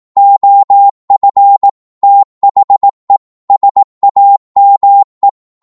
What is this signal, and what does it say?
Morse code, heard as a keyed tone, about 18 wpm, 800 Hz. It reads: OF THE SAME
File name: Morse / psg-18wpm